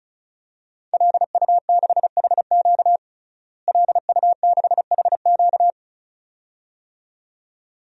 Morse code, heard as a keyed tone, about 34 wpm, 700 Hz. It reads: LU6HQ LU6HQ
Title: DXCC-100 Call Signs - 5 Characters - Repeated 2x - 35wpm